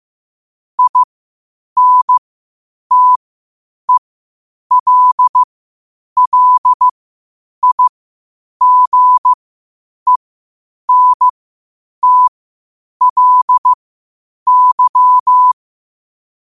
Morse code, heard as a keyed tone, about 15 wpm, spaced as for 8 wpm, 1000 Hz. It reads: INTELLIGENTLY
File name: qotdfiles / 8wpm